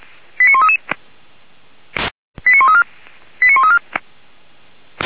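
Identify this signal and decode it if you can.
Five-tone selective call (ZVEI), heard as quick runs of five tones, tones of about 70 ms. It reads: 89133 89134 89134
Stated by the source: ZVEI five-tone selective call of a fire brigade, received on an SDR.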